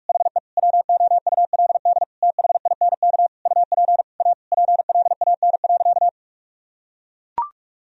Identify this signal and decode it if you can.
Morse code, heard as a keyed tone, about 41 wpm, 700 Hz. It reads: HE WOULD THINK UP A PLAN.
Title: Sentences from Top 1000 Words - Review Part 2 - 45wpm